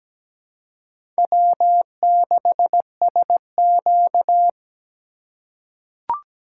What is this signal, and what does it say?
Morse code, heard as a keyed tone, about 17 wpm, 700 Hz. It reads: W6SQ
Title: USA Call Signs - 4 Characters - 3x Character Spacing - 17wpm